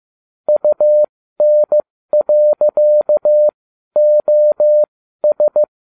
Morse code, heard as a keyed tone, about 15 wpm, spaced as for 12 wpm, 600 Hz. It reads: UN.OS